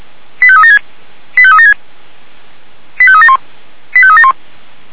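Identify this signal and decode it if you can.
Five-tone selective call (ZVEI), heard as quick runs of five tones, tones of about 70 ms. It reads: 85376 85376 85371 85371
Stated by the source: ZVEI five-tone selective call 89145, received by SDR.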